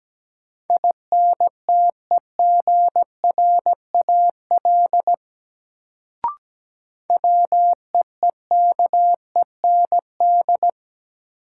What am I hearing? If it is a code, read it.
Morse code, heard as a keyed tone, about 17 wpm, 700 Hz. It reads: INTEGRAL WEEKEND
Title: Sets of 5 Words - New Word Drill - 17wpm